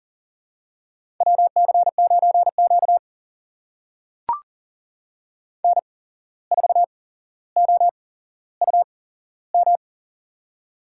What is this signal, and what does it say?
Morse code, heard as a keyed tone, about 40 wpm, spaced as for 16 wpm, 700 Hz. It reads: WC9Q N4OUM
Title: USA Call Signs - 8x Character Spacing - 40wpm